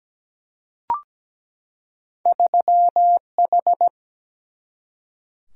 Morse code, heard as a keyed tone, about 17 wpm, 700 Hz. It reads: 3H
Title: Single-Letter Single-Number 17wpm